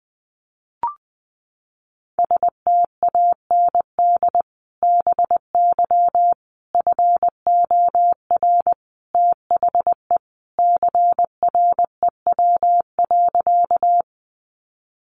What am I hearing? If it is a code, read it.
Morse code, heard as a keyed tone, about 20 wpm, 700 Hz. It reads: STAND BY FOR THE CREW.
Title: Sentences from Top 1500 Words 20wpm